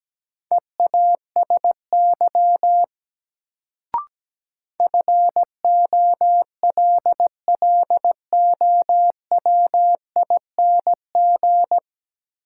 Morse code, heard as a keyed tone, about 17 wpm, 700 Hz. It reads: EASY FOLLOWING
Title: Sentences from Top 700 Words - New Word Drill - 17wpm